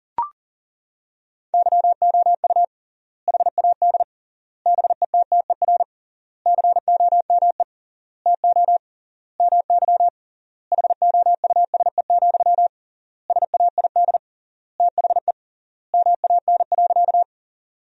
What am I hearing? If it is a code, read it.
Morse code, heard as a keyed tone, about 40 wpm, 700 Hz. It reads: YOU HAD BETTER COME TO MY HOUSE, SAID THE MAN.
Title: 3x Word-Spacing - Sentences from Top 400 Words - 40wpm